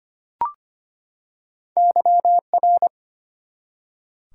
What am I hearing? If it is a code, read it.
Morse code, heard as a keyed tone, about 25 wpm, 700 Hz. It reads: YR
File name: Two Letter-Number Combinations - 25wpm